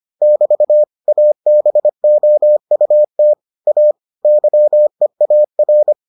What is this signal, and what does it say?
Morse code, heard as a keyed tone, about 25 wpm, 600 Hz. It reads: <BT> ABOUT A YEAR